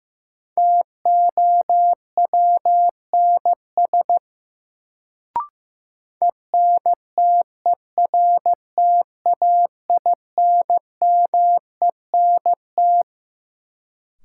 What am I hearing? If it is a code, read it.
Morse code, heard as a keyed tone, about 15 wpm, 700 Hz. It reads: TOWNS ENTERTAINMENT
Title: Sentences from Top 1000 Words - Review Part 6 - New Word Drill - 15wpm